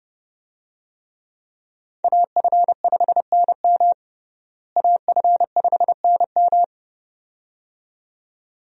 Morse code, heard as a keyed tone, about 32 wpm, 700 Hz. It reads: AF5NM AF5NM
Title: USA Call Signs - 5 Characters - Repeated 2x - 30wpm